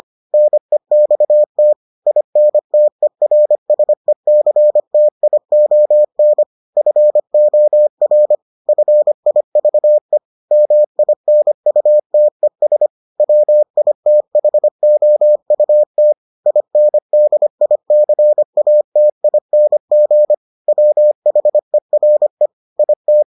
Morse code, heard as a keyed tone, about 25 wpm, 600 Hz. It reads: NEXT INTERSECTION FOR FIVE MINUTES WITHOUT INDICATING WHERE IT